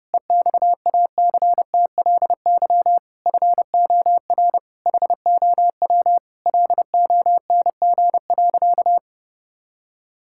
Morse code, heard as a keyed tone, about 30 wpm, 700 Hz. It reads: EXACTLY FOR HOW LONG.
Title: Sentences from Top 1000 Words - Review Part 6 - 30wpm